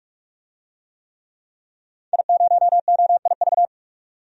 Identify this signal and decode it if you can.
Morse code, heard as a keyed tone, about 46 wpm, 700 Hz. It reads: I0OIV
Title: DXCC-100 Call Signs - 5 Characters - 45wpm